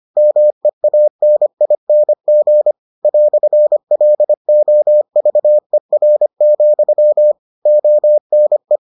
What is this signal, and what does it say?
Morse code, heard as a keyed tone, about 25 wpm, 600 Hz. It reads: MEANING "LOVER, ONE